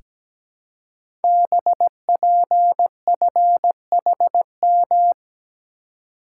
Morse code, heard as a keyed tone, about 17 wpm, 700 Hz. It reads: BPFHM
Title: Code Groups - 5 Letters - 17wpm